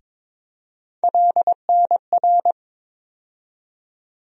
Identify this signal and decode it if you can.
Morse code, heard as a keyed tone, about 22 wpm, 700 Hz. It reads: LNR
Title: USA Call Sign Postfixes - 3 Letters 22wpm